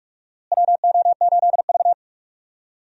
Morse code, heard as a keyed tone, about 43 wpm, 700 Hz. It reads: WO8V
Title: USA Call Signs - 4 Characters - Repeated 2x - 45wpm